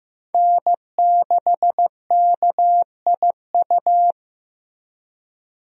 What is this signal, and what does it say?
Morse code, heard as a keyed tone, about 15 wpm, 700 Hz. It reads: N6KIU
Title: USA Call Signs - 5 Characters - Repeated 3x - 15wpm